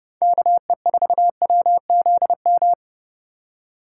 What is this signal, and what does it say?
Morse code, heard as a keyed tone, about 29 wpm, 700 Hz. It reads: KE4WZM